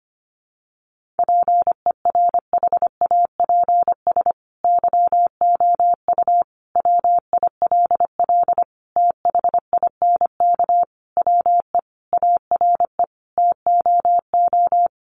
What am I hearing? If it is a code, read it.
Morse code, heard as a keyed tone, about 25 wpm, 700 Hz. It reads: PERHAPS YOU WILL THINK WE ARE TOO
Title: Sentences from Top 1400 Words 25wpm